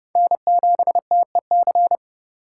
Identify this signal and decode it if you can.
Morse code, heard as a keyed tone, about 30 wpm, 700 Hz. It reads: N7TEC
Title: USA Call Signs - 5 Characters - 2x Character Spacing - 30wpm